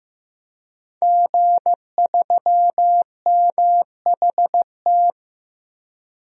Morse code, heard as a keyed tone, about 15 wpm, 700 Hz. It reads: G3MHT
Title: DXCC-100 Call Signs - 5 Characters - Repeated 3x - 15wpm